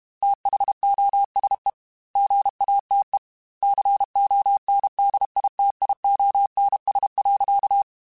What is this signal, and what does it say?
Morse code, heard as a keyed tone, about 32 wpm, 800 Hz. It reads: THOSE GATE CONDITIONS.
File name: Technical Subjects / Technical Subjects-mp3s / AND Gate @30